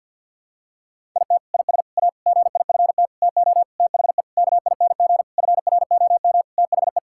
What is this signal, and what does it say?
Morse code, heard as a keyed tone, about 50 wpm, 700 Hz. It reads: IT IS A GIFT TO THE KING FROM THE
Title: Sentences from Top 1000 Words - Review Part 6 - 50wpm